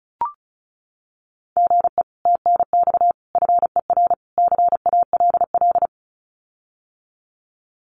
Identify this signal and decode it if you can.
Morse code, heard as a keyed tone, about 34 wpm, 700 Hz. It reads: GE TNX FER CALL
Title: Common QSO Phrases Encore 35wpm